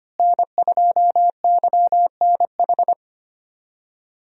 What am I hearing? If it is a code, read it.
Morse code, heard as a keyed tone, about 25 wpm, 700 Hz. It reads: N2YNH
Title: USA Call Signs - 5 Characters - Repeated 3x - 25wpm